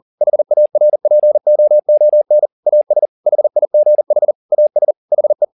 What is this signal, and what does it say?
Morse code, heard as a keyed tone, about 40 wpm, 600 Hz. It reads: HARPOON AS HIGH AS HE
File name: the old man and the sea 2 40wpm